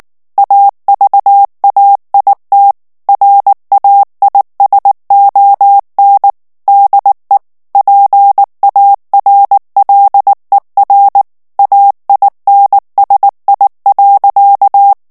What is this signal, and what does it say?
Morse code, heard as a keyed tone, about 19 wpm, 800 Hz. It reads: AVAIT RAISON DE PARLER AINSI.
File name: CW / FICHIERS-MP3 / FRANCAIS TEXTE CLAIR 2 VITESSE 18 MOTS MINUTES